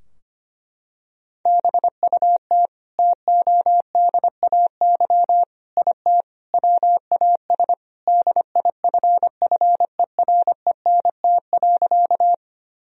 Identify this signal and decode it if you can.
Morse code, heard as a keyed tone, about 25 wpm, 700 Hz. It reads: BUT TODAY IT WAS DIFFERENT.
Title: Sentences from Top 1000 Words - Review Part 4 - 25wpm